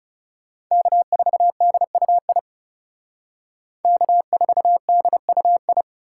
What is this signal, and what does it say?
Morse code, heard as a keyed tone, about 38 wpm, 700 Hz. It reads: K4DUI K4DUI